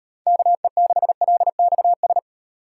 Morse code, heard as a keyed tone, about 37 wpm, 700 Hz. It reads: KE6LXS